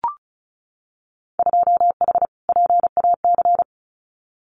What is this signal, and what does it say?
Morse code, heard as a keyed tone, about 36 wpm, 700 Hz. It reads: 2H PAC
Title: Winter Field Day Exchanges Encore 35wpm